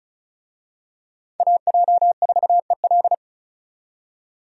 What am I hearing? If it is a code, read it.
Morse code, heard as a keyed tone, about 36 wpm, 700 Hz. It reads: AJ4EL
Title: USA Call Signs - 5 Characters - 35wpm